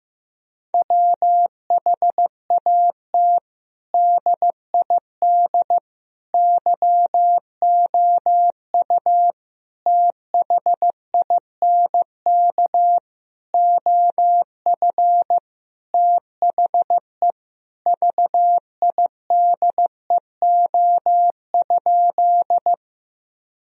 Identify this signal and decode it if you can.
Morse code, heard as a keyed tone, about 15 wpm, 700 Hz. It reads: WHAT DID YOU THINK OF THE VIDEO?
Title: Sentences from Top 1000 Words - Review Part 4 - 15wpm